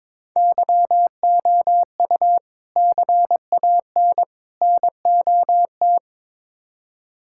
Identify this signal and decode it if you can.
Morse code, heard as a keyed tone, about 22 wpm, 700 Hz. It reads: YOU CAN NOT
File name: practice / 22wpm / Build A Phrase - 5-Word Phrases - 22wpm